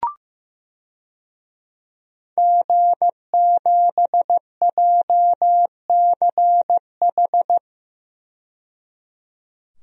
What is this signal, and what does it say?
Morse code, heard as a keyed tone, about 15 wpm, 700 Hz. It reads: G7JCH
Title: DXCC-100 Call Signs - 5 Characters - 15wpm